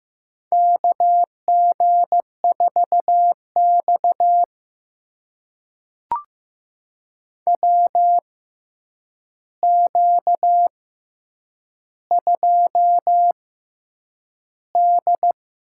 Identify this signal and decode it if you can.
Morse code, heard as a keyed tone, about 15 wpm, spaced as for 8 wpm, 700 Hz. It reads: KG4X WQ2D
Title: USA Call Signs - 4 Characters - 6x Character Spacing - 15wpm